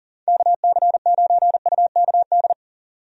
Morse code, heard as a keyed tone, about 40 wpm, 700 Hz. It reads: KC9UKD
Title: USA Call Signs - 6 Characters - 40wpm